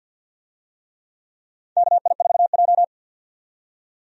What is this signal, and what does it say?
Morse code, heard as a keyed tone, about 45 wpm, 700 Hz. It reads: KI4J
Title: USA Call Signs - 4 Characters - 50wpm